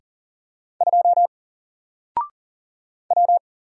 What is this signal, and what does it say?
Morse code, heard as a keyed tone, about 40 wpm, 700 Hz. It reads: 2 W
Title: Single Letter-Number Alt 40wpm